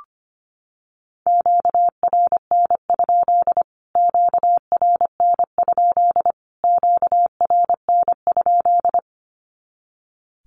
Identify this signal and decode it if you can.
Morse code, heard as a keyed tone, about 25 wpm, 700 Hz. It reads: QRN? QRN? QRN?
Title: Q-Codes Repeated 3x 25wpm